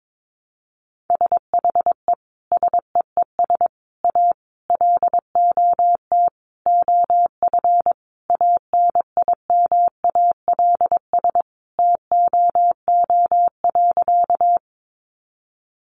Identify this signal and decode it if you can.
Morse code, heard as a keyed tone, about 22 wpm, 700 Hz. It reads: SHE SEES A LOT OF ANIMALS TOO.
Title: Easy Phrases 22wpm